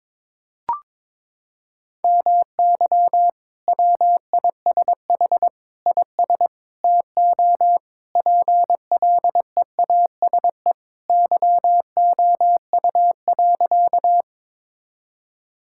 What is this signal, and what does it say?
Morse code, heard as a keyed tone, about 22 wpm, 700 Hz. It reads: MY WISH IS TO PLEASE YOU.
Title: Sentences from Top 1600 Words 22wpm